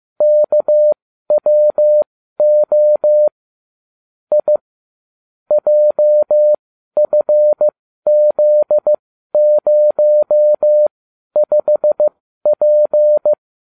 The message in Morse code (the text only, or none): KWO I JFZ05P